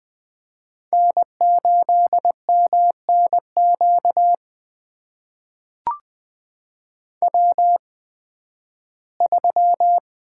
Morse code, heard as a keyed tone, about 20 wpm, spaced as for 13 wpm, 700 Hz. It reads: N8MNQ W3
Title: USA Call Signs - 8x Character Spacing - 20wpm